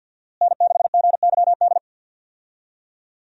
Morse code, heard as a keyed tone, about 53 wpm, 700 Hz. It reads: N6GYD
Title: USA Call Signs - 5 Characters - Repeated 2x - 50wpm